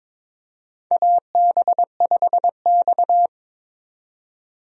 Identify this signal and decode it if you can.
Morse code, heard as a keyed tone, about 21 wpm, 700 Hz. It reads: AB5X
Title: USA Call Signs - 4 Characters - 22wpm